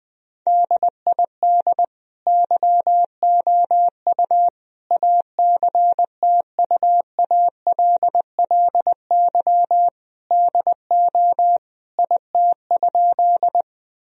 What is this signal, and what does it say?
Morse code, heard as a keyed tone, about 20 wpm, 700 Hz. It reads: DID YOU ACTUALLY DO IT?
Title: Sentences from Top 1000 Words - Review Part 4 - 20wpm